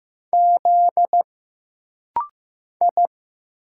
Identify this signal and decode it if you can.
Morse code, heard as a keyed tone, about 15 wpm, 700 Hz. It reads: Z I
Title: Single Letters - Rapid-Fire 15wpm